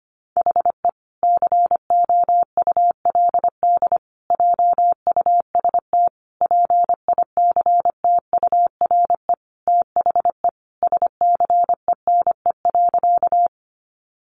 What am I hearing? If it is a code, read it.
Morse code, heard as a keyed tone, about 25 wpm, 700 Hz. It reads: HE COULD JUST PICTURE THE SCENE.
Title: Sentences from Top 1100 Words 25wpm